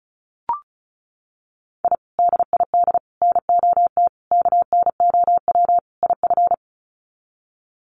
Morse code, heard as a keyed tone, about 34 wpm, 700 Hz. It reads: I DID NOT KNOW IF
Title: Sets of 5 Words Encore 35wpm